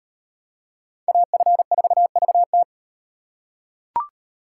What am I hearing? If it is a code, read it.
Morse code, heard as a keyed tone, about 35 wpm, 700 Hz. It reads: AF4VT